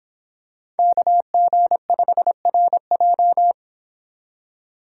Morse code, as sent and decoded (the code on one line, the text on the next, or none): -.- --. ..... .-. .---
KG5RJ